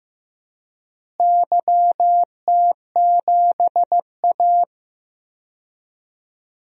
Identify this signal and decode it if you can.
Morse code, heard as a keyed tone, about 15 wpm, 700 Hz. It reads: YT7A
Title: DXCC-100 Call Signs - 4 Characters - Repeated 3x - 15wpm